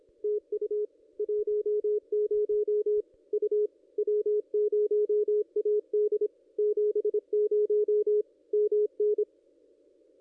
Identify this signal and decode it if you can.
Morse code, heard as a keyed tone, about 25 wpm, 440 Hz. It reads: TU 10 U W0AD 70 MN
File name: contests / ss-cw-k5tr-op-n5rz -1700